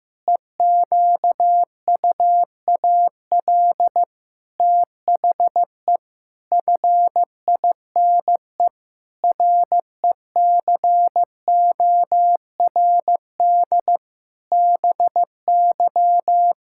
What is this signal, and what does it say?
Morse code, heard as a keyed tone, about 15 wpm, 700 Hz. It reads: EQUAL THE FINE RECORD BY
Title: Pangram Challenge v2 - 15wpm